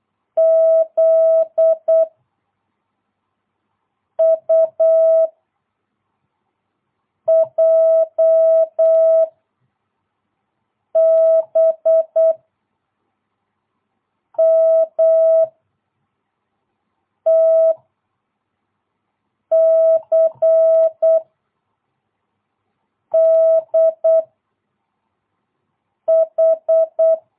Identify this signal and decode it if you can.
Morse code, heard as a keyed tone, about 8 wpm, 660 Hz. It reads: Z U J B M T C D H
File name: jh8cbh / 2022cw